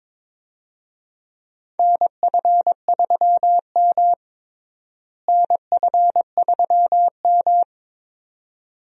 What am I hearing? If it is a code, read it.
Morse code, heard as a keyed tone, about 22 wpm, 700 Hz. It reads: NF3M NF3M
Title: USA Call Signs - 4 Characters - Repeated 2x - 22wpm